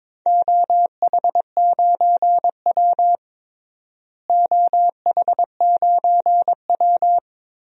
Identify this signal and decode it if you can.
Morse code, heard as a keyed tone, about 22 wpm, 700 Hz. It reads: OH9W OH9W
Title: DXCC-100 Call Signs - 4 Characters - Repeated 3x - 22wpm